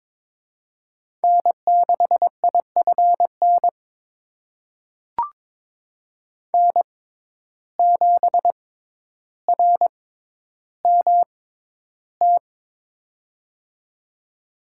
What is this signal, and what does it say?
Morse code, heard as a keyed tone, about 22 wpm, spaced as for 11 wpm, 700 Hz. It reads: N6IFN N7RMT
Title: USA Call Signs - 5 Characters - 6x Character Spacing - 22wpm